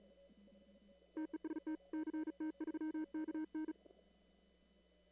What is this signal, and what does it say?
Morse code, heard as a keyed tone, about 37 wpm, 310 Hz. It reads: TEST CT3KN
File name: CW / kt5j wpx cw-op-k5pi